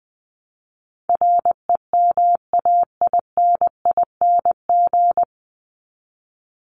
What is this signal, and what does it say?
Morse code, heard as a keyed tone, about 20 wpm, 700 Hz. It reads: REMAINING
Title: ING Suffix - Top 100 Words with Base Words 20wpm